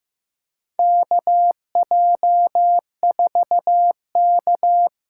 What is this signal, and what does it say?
Morse code, heard as a keyed tone, about 15 wpm, 700 Hz. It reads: KJ4K
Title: USA Call Signs - 6 Characters - 8x Character Spacing - 15wpm